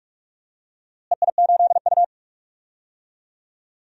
Morse code, heard as a keyed tone, about 44 wpm, 700 Hz. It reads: EI8U
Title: DXCC-100 Call Signs - 4 Characters - Repeated 2x - 45wpm